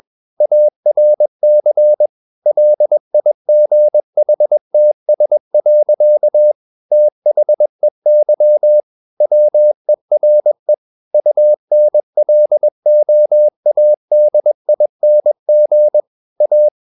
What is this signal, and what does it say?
Morse code, heard as a keyed tone, about 21 wpm, 600 Hz. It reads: ARC LIGHTS. THEY WERE UNLOADING A
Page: CW med Gnister